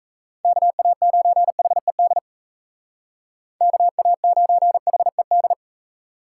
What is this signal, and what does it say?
Morse code, heard as a keyed tone, about 33 wpm, 700 Hz. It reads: KA9HED KA9HED
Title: Speed Racing - USA 2x3 Call Signs 28wpm